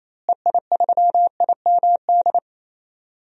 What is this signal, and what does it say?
Morse code, heard as a keyed tone, about 27 wpm, 700 Hz. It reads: EI3IMD
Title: DXCC-100 Call Signs - 6 Characters - Repeated 2x - 28wpm